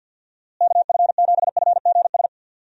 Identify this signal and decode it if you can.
Morse code, heard as a keyed tone, about 52 wpm, 700 Hz. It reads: KF7FGS
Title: USA Call Signs - 6 Characters - Repeated 3x - 50wpm